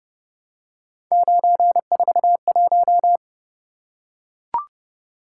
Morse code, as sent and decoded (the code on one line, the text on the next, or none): ----. ....- .----
941